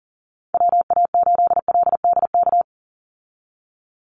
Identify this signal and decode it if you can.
Morse code, heard as a keyed tone, about 40 wpm, 700 Hz. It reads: WA8LDK
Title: USA Call Signs 40wpm